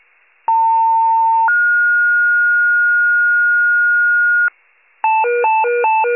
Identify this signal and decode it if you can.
Two-tone page, heard as a one-second tone, then a three-second tone, about 900 then 1430 Hz, with noise. It is heard from 0.5 to 4.5 s.